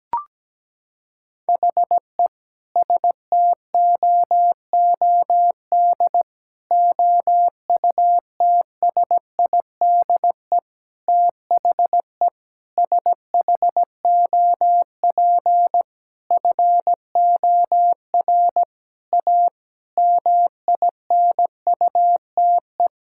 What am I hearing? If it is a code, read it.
Morse code, heard as a keyed tone, about 17 wpm, 700 Hz. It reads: HE STOOD OUTSIDE THE SHOP FOR A MINUTE
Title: Sentences from Top 1600 Words 17wpm